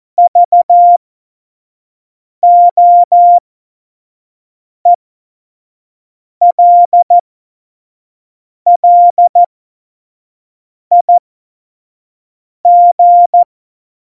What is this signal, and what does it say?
Morse code, heard as a keyed tone, about 14 wpm, spaced as for 5 wpm, 700 Hz. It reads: VOELLIG
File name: Morsekurs / Witze / Witz 03.Dezember 2019, 25 F70